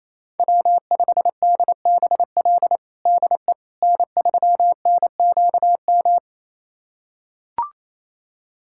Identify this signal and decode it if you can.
Morse code, heard as a keyed tone, about 28 wpm, 700 Hz. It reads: W5DBL DE N3NQM
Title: USA Callsign DE Callsign - 5 Characters - 28wpm